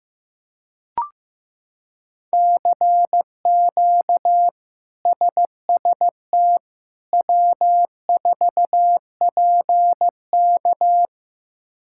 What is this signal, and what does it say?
Morse code, heard as a keyed tone, about 16 wpm, 700 Hz. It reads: CQ SST W4PK
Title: K1USN SST QSO Practice - 15wpm